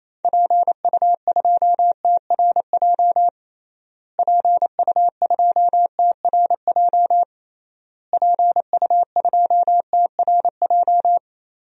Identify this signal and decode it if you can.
Morse code, heard as a keyed tone, about 28 wpm, 700 Hz. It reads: PU2TRJ PU2TRJ PU2TRJ
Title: DXCC-100 Call Signs - 6 Characters - Repeated 3x - 28wpm